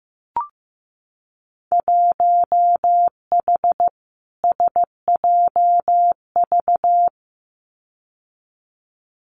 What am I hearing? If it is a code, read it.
Morse code, heard as a keyed tone, about 15 wpm, 700 Hz. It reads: 1H SJV
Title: Winter Field Day Exchanges 15wpm